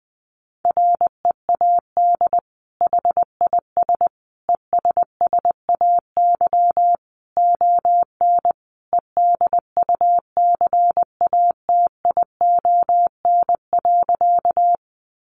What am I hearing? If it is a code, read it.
Morse code, heard as a keyed tone, about 19 wpm, 700 Hz. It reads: READ HIS ESSAY ON EDUCATION.
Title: Sentences from Top 1500 Words 20wpm